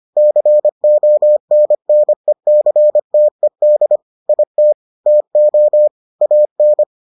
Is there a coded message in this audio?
Yes: Morse code from 0.2 to 6.8 s, heard as a keyed tone, about 25 wpm, 600 Hz. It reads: CONNECTED IT TO AN